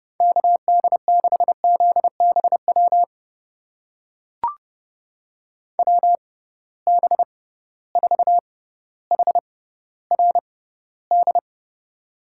Morse code, heard as a keyed tone, about 30 wpm, spaced as for 15 wpm, 700 Hz. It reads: KD6ZBW WB4HRD
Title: USA Call Signs - 6 Characters - 6x Character Spacing - 30wpm